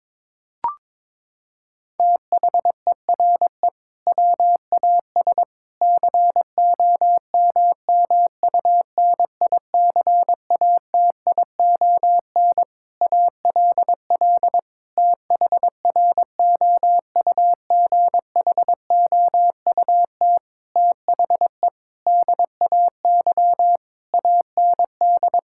Morse code, heard as a keyed tone, about 22 wpm, 700 Hz. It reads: THERE WAS COMMUNICATION ALL THROUGHOUT THE DAY AND
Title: Sentences from Top 1000 Words - Review Part 6 - 22wpm